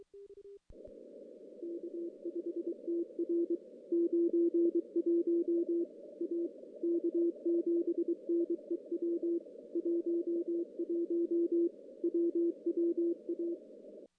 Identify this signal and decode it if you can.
Morse code, heard as a keyed tone, about 23 wpm, 340 Hz. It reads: K5TR 91 A K7NEW 11 WWA
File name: contests / CW-SS-k5tr- -0000